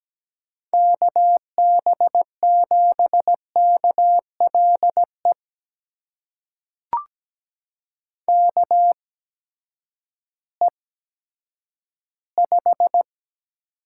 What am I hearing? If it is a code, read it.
Morse code, heard as a keyed tone, about 17 wpm, spaced as for 10 wpm, 700 Hz. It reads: KB7KLE KE5
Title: USA Call Signs - 6 Characters - 8x Character Spacing - 17wpm